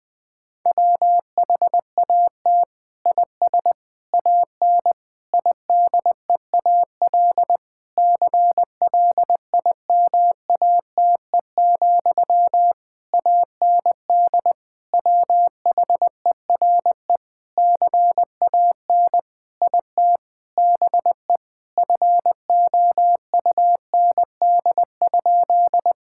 Morse code, heard as a keyed tone, about 20 wpm, 700 Hz. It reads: WHAT IS AN IDEAL CLIMATE, AND WHERE CAN IT BE FOUND?
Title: Sentences from Top 1600 Words 20wpm